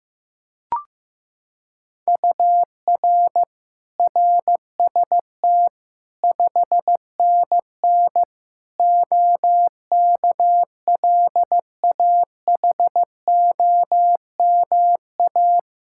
Morse code, heard as a keyed tone, about 15 wpm, 700 Hz. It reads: UR RST 5NN OKLAHOMA
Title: Practice QSO - Exchange 1 - Beginning - 15wpm